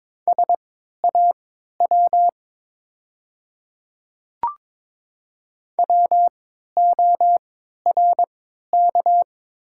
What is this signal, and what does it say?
Morse code, heard as a keyed tone, about 22 wpm, spaced as for 13 wpm, 700 Hz. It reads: SAW WORK
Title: 3x Character Spacing - Top 300 Words - 22wpm